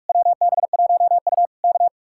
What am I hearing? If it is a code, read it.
Morse code, heard as a keyed tone, about 49 wpm, 700 Hz. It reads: WB1U K